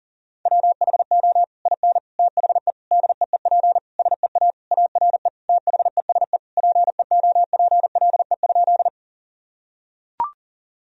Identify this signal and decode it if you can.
Morse code, heard as a keyed tone, about 37 wpm, 700 Hz. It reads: WHO IN THE DEEP SEA ARE THESE PEOPLE?
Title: Sentences from Top 1000 Words - Review Part 3 - 40wpm